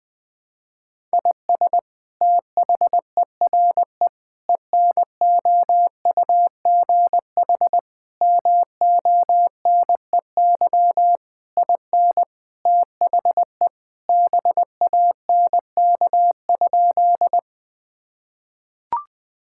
Morse code, heard as a keyed tone, about 20 wpm, 700 Hz. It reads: IS THERE ENOUGH MONEY IN THE BANK?
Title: Sentences from Top 1000 Words - Review Part 6 - 20wpm